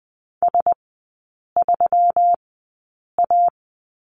Morse code, heard as a keyed tone, about 20 wpm, spaced as for 8 wpm, 700 Hz. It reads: S3A
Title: 127 Lesson 21e - Number 6 - Three Characters 20wpm